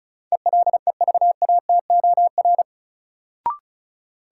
Morse code, heard as a keyed tone, about 34 wpm, 700 Hz. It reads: ELEVATOR
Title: Sentences from Top 1600 Words - New Word Drill - 35wpm